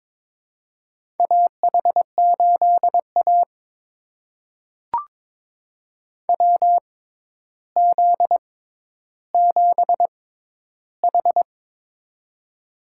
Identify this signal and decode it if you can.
Morse code, heard as a keyed tone, about 22 wpm, spaced as for 11 wpm, 700 Hz. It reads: AH8A WZ7H
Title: USA Call Signs - 4 Characters - 6x Character Spacing - 22wpm